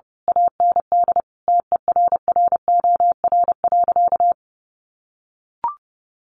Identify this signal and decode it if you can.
Morse code, heard as a keyed tone, about 30 wpm, 700 Hz. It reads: AND TERROR.
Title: A Sound Of Thunder 30wpm